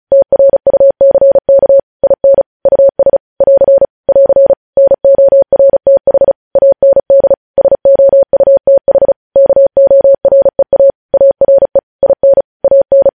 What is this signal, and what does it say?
Morse code, heard as a keyed tone, about 36 wpm, 550 Hz. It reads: TRUCK IN US <AR> <AR> NORTH AND SOUTH KOREA ARE IN AN
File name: podcast / News headlines-1x-35x35